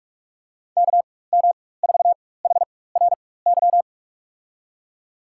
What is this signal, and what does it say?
Morse code, heard as a keyed tone, about 41 wpm, spaced as for 20 wpm, 700 Hz. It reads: KM4HRY
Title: USA Call Signs - 6 Characters - 4x Character Spacing - 45wpm